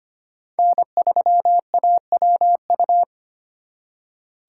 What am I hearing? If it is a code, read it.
Morse code, heard as a keyed tone, about 32 wpm, 700 Hz. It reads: N3AWU